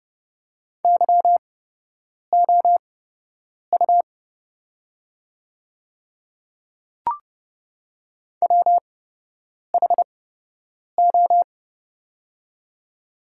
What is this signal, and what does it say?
Morse code, heard as a keyed tone, about 30 wpm, spaced as for 8 wpm, 700 Hz. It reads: YOU WHO
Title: 8x Character Spacing - Top 100 Words - 30wpm